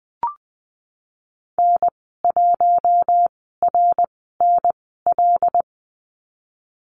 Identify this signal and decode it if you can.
Morse code, heard as a keyed tone, about 20 wpm, spaced as for 15 wpm, 700 Hz. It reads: N1RNL
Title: 179 Lesson 29g - Slash - Call Signs 20wpm